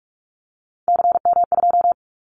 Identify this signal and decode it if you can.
Morse code, heard as a keyed tone, about 46 wpm, 700 Hz. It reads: CM2
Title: Callsign Prefixes for Top 100 DXCC - 45wpm